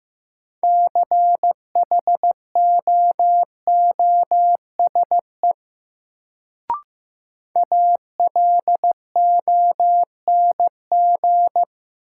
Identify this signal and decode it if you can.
Morse code, heard as a keyed tone, about 15 wpm, 700 Hz. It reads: CHOOSE ALONG
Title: Sentences from Top 700 Words - New Word Drill - 15wpm